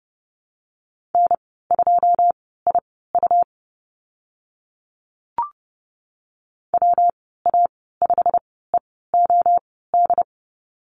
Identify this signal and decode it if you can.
Morse code, heard as a keyed tone, about 30 wpm, spaced as for 17 wpm, 700 Hz. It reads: N2IU WA5EOD
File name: Lesson 19g - Letter w - Call Signs 30wpm